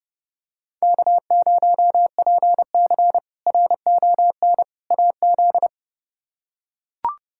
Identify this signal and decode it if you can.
Morse code, heard as a keyed tone, about 30 wpm, 700 Hz. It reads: K0PC RON AZ
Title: K1USN SST QSO Practice - 30wpm